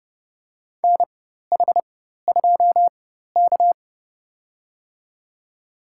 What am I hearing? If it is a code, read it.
Morse code, heard as a keyed tone, about 32 wpm, spaced as for 14 wpm, 700 Hz. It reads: NH2K
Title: USA Call Signs - 4 Characters - 4x Character Spacing - 30wpm